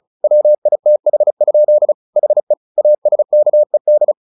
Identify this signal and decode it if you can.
Morse code, heard as a keyed tone, about 34 wpm, 600 Hz. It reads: WITH? HE ASKED
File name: the old man and the sea 2 35wpm